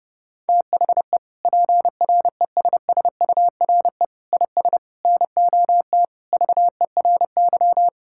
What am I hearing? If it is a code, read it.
Morse code, heard as a keyed tone, about 30 wpm, 700 Hz. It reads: THE PRESSURE IS NOT VERY
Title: Sentences from Top 1000 Words - Review Part 6 - 30wpm